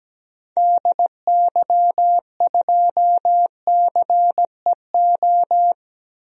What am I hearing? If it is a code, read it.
Morse code, heard as a keyed tone, about 17 wpm, 700 Hz. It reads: DY2CEO
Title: DXCC-100 Call Signs - 6 Characters - Repeated 3x - 17wpm